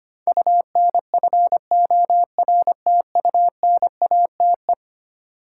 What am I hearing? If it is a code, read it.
Morse code, heard as a keyed tone, about 25 wpm, 700 Hz. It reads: UNFORTUNATE
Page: morse-code-ninja-practice-25wpm